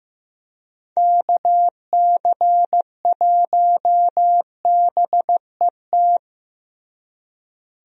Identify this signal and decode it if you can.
Morse code, heard as a keyed tone, about 15 wpm, 700 Hz. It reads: KC1BET
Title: USA Call Signs - 6 Characters - Repeated 2x - 15wpm